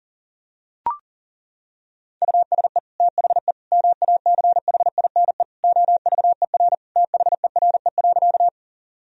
Morse code, heard as a keyed tone, about 40 wpm, 700 Hz. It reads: USE THE MACHINE OVER THERE.
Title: Sentences from Top 1000 Words - Review Part 2 - 40wpm